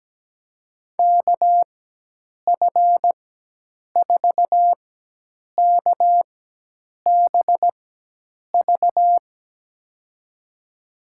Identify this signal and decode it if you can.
Morse code, heard as a keyed tone, about 17 wpm, spaced as for 8 wpm, 700 Hz. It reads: KF4KBV
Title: USA Call Signs - 6 Characters - 4x Character Spacing - 17wpm